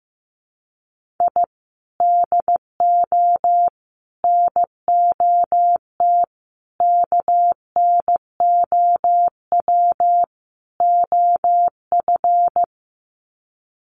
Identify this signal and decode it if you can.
Morse code, heard as a keyed tone, about 15 wpm, 700 Hz. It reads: I DO NOT KNOW OF
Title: Sets of 5 Words Encore2 15wpm